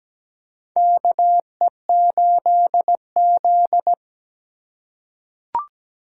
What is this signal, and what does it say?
Morse code, heard as a keyed tone, about 17 wpm, 700 Hz. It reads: KE8Z
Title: USA Call Signs - 4 Characters - 3x Character Spacing - 17wpm